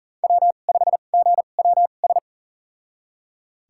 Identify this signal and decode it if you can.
Morse code, heard as a keyed tone, about 38 wpm, spaced as for 28 wpm, 700 Hz. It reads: W5GWS
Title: USA Call Signs - 5 Characters - 2x Character Spacing - 40wpm